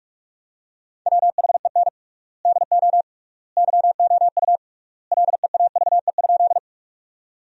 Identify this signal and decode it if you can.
Morse code, heard as a keyed tone, about 42 wpm, 700 Hz. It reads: WHEN DO YOU LEAVE?
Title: 3x Word-Spacing - Sentences from Top 400 Words - 45wpm